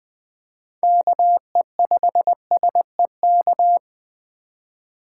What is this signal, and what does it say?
Morse code, heard as a keyed tone, about 20 wpm, 700 Hz. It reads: KE5SEK
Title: Speed Racing - USA 2x3 Call Signs 17wpm